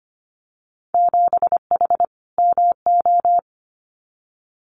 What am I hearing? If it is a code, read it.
Morse code, heard as a keyed tone, about 25 wpm, 700 Hz. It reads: 7H MO
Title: Winter Field Day Exchanges Encore 25wpm